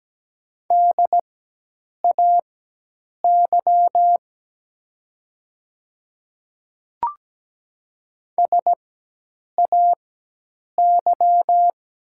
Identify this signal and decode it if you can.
Morse code, heard as a keyed tone, about 17 wpm, spaced as for 8 wpm, 700 Hz. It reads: DAY SAY
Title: 4x Character Spacing - Top 200 Words - 17wpm